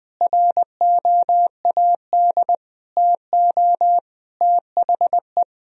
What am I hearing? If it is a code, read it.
Morse code, heard as a keyed tone, about 20 wpm, 700 Hz. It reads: ROAD TO THE
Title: Sentences from Top 1000 Words - Review Part 2 - 20wpm